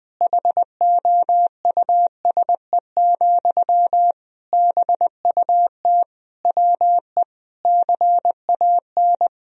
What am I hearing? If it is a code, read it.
Morse code, heard as a keyed tone, about 20 wpm, 700 Hz. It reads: HOUSE, BUT WE CAN
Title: Sentences from Top 1000 Words - Review Part 1 - 20wpm